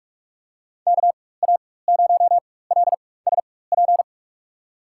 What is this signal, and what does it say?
Morse code, heard as a keyed tone, about 47 wpm, spaced as for 21 wpm, 700 Hz. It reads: KA0LSP
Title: USA Call Signs - 6 Characters - 4x Character Spacing - 45wpm